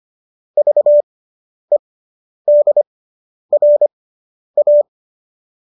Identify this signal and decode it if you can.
Morse code, heard as a keyed tone, about 25 wpm, spaced as for 10 wpm, 590 Hz. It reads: VEDRA